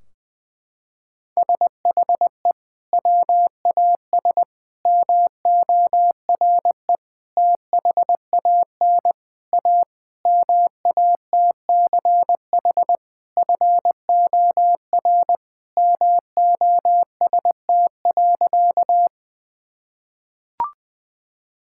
Morse code, heard as a keyed tone, about 19 wpm, 700 Hz. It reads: SHE WAS MORE THAN A MATCH FOR MOST.
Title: Sentences from Top 1600 Words 20wpm